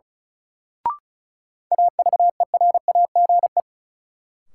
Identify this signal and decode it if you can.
Morse code, heard as a keyed tone, about 36 wpm, 700 Hz. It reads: AVERAGE
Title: Sentences from Top 1400 Words - New Word Drill Alt - 35wpm